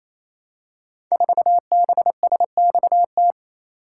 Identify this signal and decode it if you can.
Morse code, heard as a keyed tone, about 28 wpm, 700 Hz. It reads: 4BSXT
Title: Code Groups - 5 Letter:Numbers - 28wpm